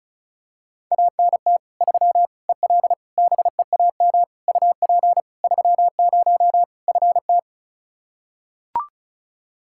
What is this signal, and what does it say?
Morse code, heard as a keyed tone, about 36 wpm, 700 Hz. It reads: ANT 3 EL BEAM UP 30 FT
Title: Practice QSO - Exchange 2 - Info - 35wpm